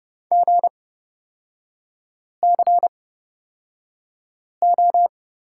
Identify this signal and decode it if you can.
Morse code, heard as a keyed tone, about 29 wpm, spaced as for 5 wpm, 690 Hz. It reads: GCO
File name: course / 30:10wpm / 236 Lesson 38e - Letter q - Three Characters 30wpm